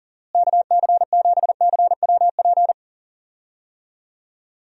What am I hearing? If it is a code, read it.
Morse code, heard as a keyed tone, about 40 wpm, 700 Hz. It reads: KC7CWP
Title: USA Call Signs - 6 Characters - Repeated 3x - 40wpm